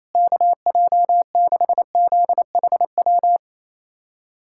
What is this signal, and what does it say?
Morse code, heard as a keyed tone, about 28 wpm, 700 Hz. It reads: KJ6ZHW
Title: USA Callsign DE Callsign - 6 Characters - 28wpm